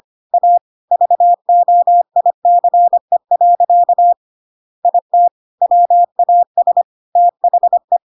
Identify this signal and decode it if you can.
Morse code, heard as a keyed tone, about 25 wpm, 700 Hz. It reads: A VOICE. IT WAS THE